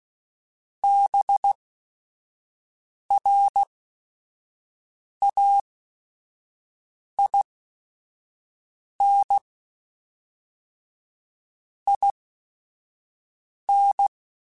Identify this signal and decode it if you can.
Morse code, heard as a keyed tone, about 16 wpm, spaced as for 5 wpm, 800 Hz. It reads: BRAIN IN
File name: Chuckles / Chuckles-mp3s / BrainMouth @5